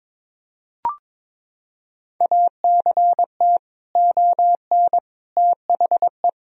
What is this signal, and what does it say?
Morse code, heard as a keyed tone, about 22 wpm, 700 Hz. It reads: ACT ON THE
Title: Sentences from Top 1000 Words - Review Part 2 - 22wpm